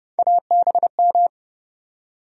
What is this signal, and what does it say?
Morse code, heard as a keyed tone, about 30 wpm, 700 Hz. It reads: ABM
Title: USA Call Sign Postfixes - 3 Letters 30wpm